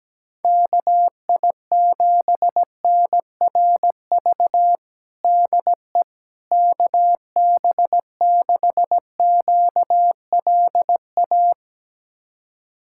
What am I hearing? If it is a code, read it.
Morse code, heard as a keyed tone, about 17 wpm, 700 Hz. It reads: KI7NRV DE KB6QLA
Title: USA Callsign DE Callsign 17wpm